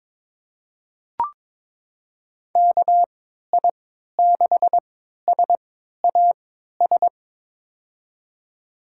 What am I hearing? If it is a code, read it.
Morse code, heard as a keyed tone, about 22 wpm, spaced as for 13 wpm, 700 Hz. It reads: KI6SAS
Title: USA Call Signs - 6 Characters - 3x Character Spacing - 22wpm